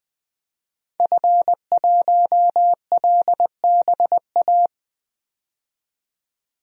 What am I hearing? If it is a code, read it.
Morse code, heard as a keyed tone, about 21 wpm, 700 Hz. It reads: F1LBA
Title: DXCC-100 Call Signs - 5 Characters - 20wpm